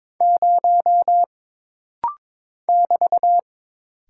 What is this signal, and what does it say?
Morse code, heard as a keyed tone, about 22 wpm, 700 Hz. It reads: 0 <BT>
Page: This rapid-fire timing is geared to push your ICR (Instant Character Recognition) skills.